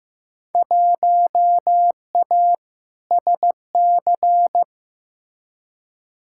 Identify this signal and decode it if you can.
Morse code, heard as a keyed tone, about 16 wpm, 700 Hz. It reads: 1A SC
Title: Field Day Exchanges - 15wpm